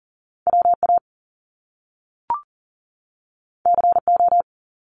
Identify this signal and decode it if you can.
Morse code, heard as a keyed tone, about 42 wpm, 700 Hz. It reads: WA CO
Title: US States 40wpm